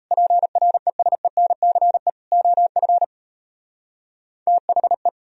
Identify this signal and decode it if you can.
Morse code, heard as a keyed tone, about 35 wpm, 700 Hz. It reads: PRESENCE OF THE